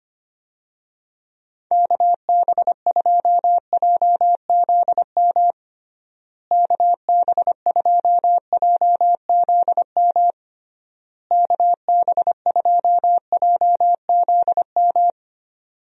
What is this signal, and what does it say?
Morse code, heard as a keyed tone, about 25 wpm, 700 Hz. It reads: KB2JZM KB2JZM KB2JZM
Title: USA Call Signs - 6 Characters - Repeated 3x - 25wpm